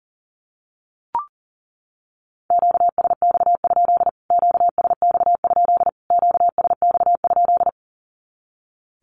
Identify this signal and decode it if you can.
Morse code, heard as a keyed tone, about 40 wpm, 700 Hz. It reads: QSX? QSX? QSX?
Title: Q-Codes Repeated 3x 40wpm